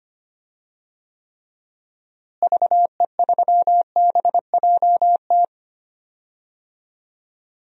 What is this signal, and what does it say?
Morse code, heard as a keyed tone, about 25 wpm, 700 Hz. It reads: VE3BJT
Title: DXCC-100 Call Signs - 6 Characters - Repeated 3x - 25wpm